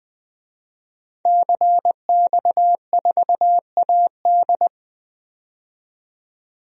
Morse code, heard as a keyed tone, about 20 wpm, 700 Hz. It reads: CX4AD
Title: DXCC-100 Call Signs - 5 Characters - 20wpm